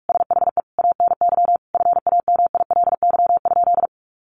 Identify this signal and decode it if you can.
Morse code, heard as a keyed tone, about 44 wpm, 700 Hz. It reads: SHE ANY FAMILY?
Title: Sentences from Top 400 Words 45wpm